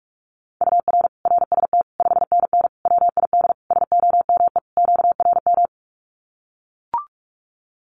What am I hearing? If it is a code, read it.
Morse code, heard as a keyed tone, about 44 wpm, 700 Hz. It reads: UR RST 5NN WID SOME QRM
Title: Common QSO Phrases 45wpm